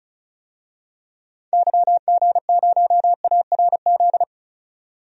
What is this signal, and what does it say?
Morse code, heard as a keyed tone, about 34 wpm, 700 Hz. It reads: YG0ARZ
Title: DXCC-100 Call Signs - 6 Characters - 35wpm